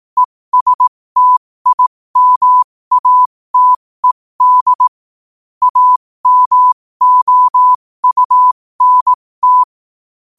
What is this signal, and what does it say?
Morse code, heard as a keyed tone, about 18 wpm, spaced as for 15 wpm, 1000 Hz. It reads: ESTIMATED AMOUNT
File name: qotdfiles / 15wpm